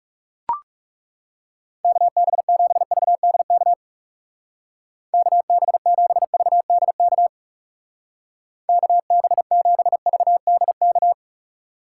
Morse code, heard as a keyed tone, about 37 wpm, 700 Hz. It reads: KB7VDK KB7VDK KB7VDK
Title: Speed Racing - USA 2x3 Call Signs 30wpm